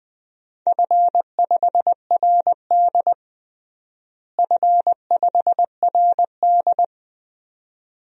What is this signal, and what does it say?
Morse code, heard as a keyed tone, about 20 wpm, 700 Hz. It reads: F5RD F5RD
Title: DXCC-100 Call Signs - 4 Characters - Repeated 2x - 20wpm